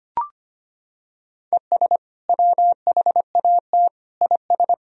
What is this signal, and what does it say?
Morse code, heard as a keyed tone, about 25 wpm, 700 Hz. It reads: ES WHAT IS
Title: Sentences from Top 100 Words and Abbreviations 25wpm